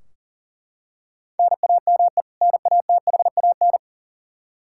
Morse code, heard as a keyed tone, about 40 wpm, 700 Hz. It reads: NAME NATHAN
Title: Practice QSO - Exchange 1 - Beginning - 40wpm